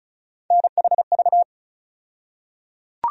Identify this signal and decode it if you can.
Morse code, heard as a keyed tone, about 34 wpm, 700 Hz. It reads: NHV
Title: USA Call Sign Postfixes - 3 Letters 35wpm